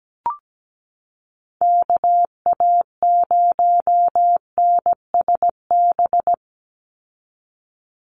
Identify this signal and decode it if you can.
Morse code, heard as a keyed tone, about 17 wpm, 700 Hz. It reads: KA0NSB
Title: USA Call Signs 17wpm